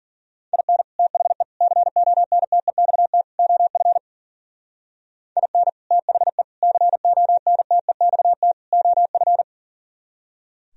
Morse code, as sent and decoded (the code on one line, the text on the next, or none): .. -. / - .... / -.-. --- -. - . -..- - / --- ..-. / .. -. / - .... / -.-. --- -. - . -..- - / --- ..-.
IN THE CONTEXT OF IN THE CONTEXT OF